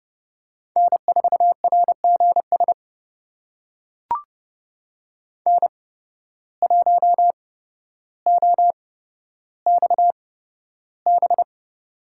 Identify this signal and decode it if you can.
Morse code, heard as a keyed tone, about 30 wpm, spaced as for 13 wpm, 700 Hz. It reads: N4RGS N1OXB
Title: USA Call Signs - 5 Characters - 8x Character Spacing - 30wpm